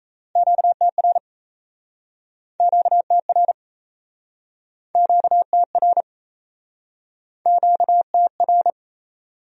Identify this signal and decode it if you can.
Morse code, heard as a keyed tone, about 39 wpm, 700 Hz. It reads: QTR QTR QTR QTR